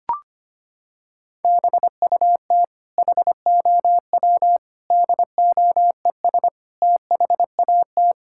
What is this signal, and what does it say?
Morse code, heard as a keyed tone, about 25 wpm, 700 Hz. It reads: BUT HOW DOES THAT
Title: Sentences from Top 1000 Words - Review Part 6 - 25wpm